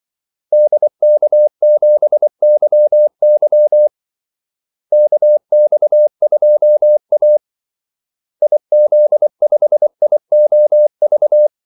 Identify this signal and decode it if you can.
Morse code, heard as a keyed tone, about 24 wpm, 600 Hz. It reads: DK7YY KX2A IZ5IOV
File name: Callsigns-24wpm